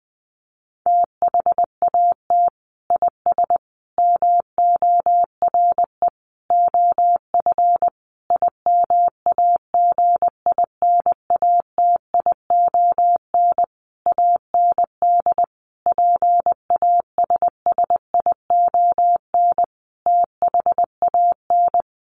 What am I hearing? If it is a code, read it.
Morse code, heard as a keyed tone, about 20 wpm, 700 Hz. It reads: THAT IS MORE OF IMAGINATION AND PASSION THAN
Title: Sentences from Top 1300 Words 20wpm